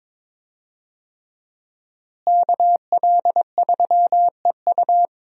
Morse code, heard as a keyed tone, about 22 wpm, 700 Hz. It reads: KL3EU
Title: USA Call Signs - 5 Characters - Repeated 3x - 22wpm